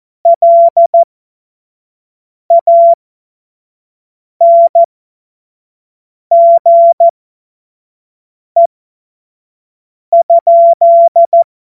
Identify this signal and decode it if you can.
Morse code, heard as a keyed tone, about 14 wpm, spaced as for 5 wpm, 680 Hz. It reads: LANGE?
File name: Morsekurs / Witze / Witz 01. Februar 2022 25 F70